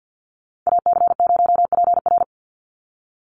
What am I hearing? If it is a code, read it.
Morse code, heard as a keyed tone, about 49 wpm, 700 Hz. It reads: AC0PR
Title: NA SOTA Chaser Callsigns 50wpm